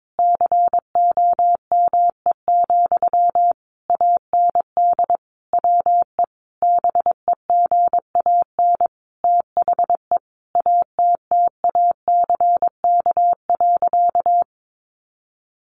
Morse code, heard as a keyed tone, about 22 wpm, 700 Hz. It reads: COME, AND WE BEGAN THE ATTACK.
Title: Sentences from Top 1500 Words 22wpm